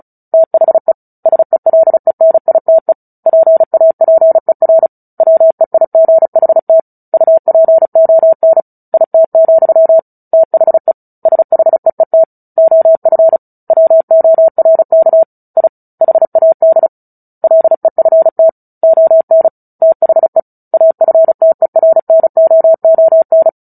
Morse code, heard as a keyed tone, about 36 wpm, 650 Hz. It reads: THE SELENITE PAPER WEIGHT UPON IT, THE SHEET OF WORK I HAD LEFT ON THE AFTERNOON
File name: Book Two / 09-Chapter 9 - 35 WPM